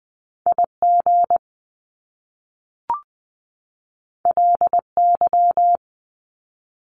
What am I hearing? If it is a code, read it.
Morse code, heard as a keyed tone, about 20 wpm, 700 Hz. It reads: IG LY
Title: Top Two Letter Combinations in Words 20wpm